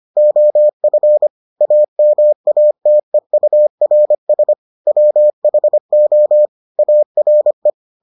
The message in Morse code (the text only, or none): OF AMATEURS WHO ARE